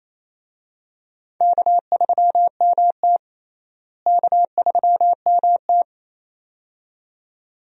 Morse code, heard as a keyed tone, about 28 wpm, 700 Hz. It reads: K3MT K3MT
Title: USA Call Signs - 4 Characters - Repeated 2x - 28wpm